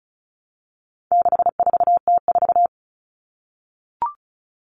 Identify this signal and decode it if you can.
Morse code, heard as a keyed tone, about 34 wpm, 700 Hz. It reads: 64T4
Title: All Cut Numbers - 4-Digit 35wpm